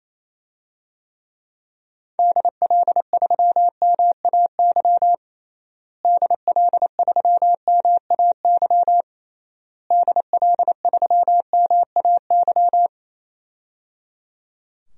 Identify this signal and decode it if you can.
Morse code, heard as a keyed tone, about 28 wpm, 700 Hz. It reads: DL3MAY DL3MAY DL3MAY
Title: DXCC-100 Call Signs - 6 Characters - Repeated 3x - 28wpm